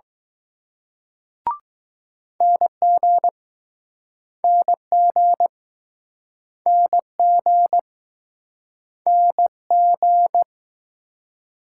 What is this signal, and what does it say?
Morse code, heard as a keyed tone, about 20 wpm, 700 Hz. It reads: NG NG NG NG